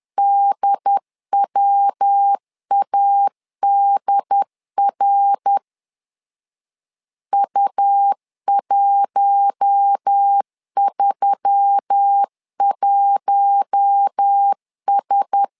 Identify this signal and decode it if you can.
Morse code, heard as a keyed tone, about 10 wpm, 800 Hz. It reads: DWADR U131S